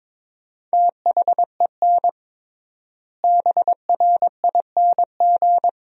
Morse code, heard as a keyed tone, about 21 wpm, 700 Hz. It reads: THEN BRING
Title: 3x Word-Spacing - Sentences from Top 500 Words - 22wpm